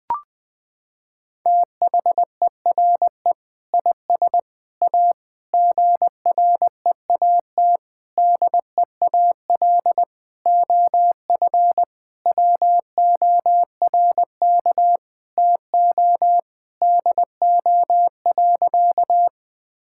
Morse code, heard as a keyed tone, about 19 wpm, 700 Hz. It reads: THERE IS A GREAT DEAL OF WORK TO DO.
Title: Sentences with 6-Word Phrases - 20wpm